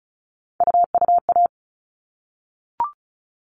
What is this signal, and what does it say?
Morse code, heard as a keyed tone, about 34 wpm, 700 Hz. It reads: UUA
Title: All Cut Numbers - 3-Digit 35wpm